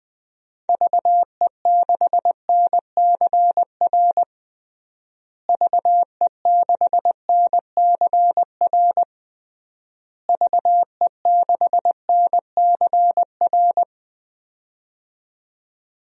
Morse code, heard as a keyed tone, about 21 wpm, 700 Hz. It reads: VE6NCR VE6NCR VE6NCR
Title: DXCC-100 Call Signs - 6 Characters - Repeated 3x - 20wpm